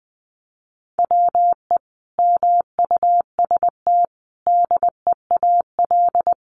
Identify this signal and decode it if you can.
Morse code, heard as a keyed tone, about 20 wpm, 700 Hz. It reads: WE MUST DEAL
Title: Sentences from Top 1400 Words 20wpm